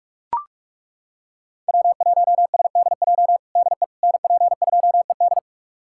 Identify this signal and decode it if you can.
Morse code, heard as a keyed tone, about 41 wpm, 700 Hz. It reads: W1SDJ DE NP2ED
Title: USA Callsign DE Callsign - 5 Characters - 45wpm